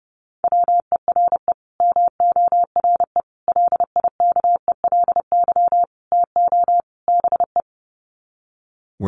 Morse code, heard as a keyed tone, about 30 wpm, 700 Hz. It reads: WERE MORE LIKELY TO BE
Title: Sets of 5 Words Encore2 30wpm